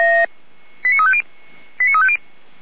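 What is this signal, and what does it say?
Five-tone selective call (ZVEI), heard as quick runs of five tones, tones of about 70 ms. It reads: 89350 89350
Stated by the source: ZVEI five-tone call 89230 received on SDR, fire brigade Orlamünde (feuerwehr-orlamuende).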